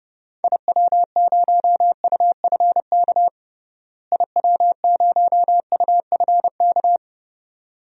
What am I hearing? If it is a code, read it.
Morse code, heard as a keyed tone, about 30 wpm, 700 Hz. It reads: IW0UFK IW0UFK
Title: DXCC-100 Call Signs - 6 Characters - Repeated 2x - 30wpm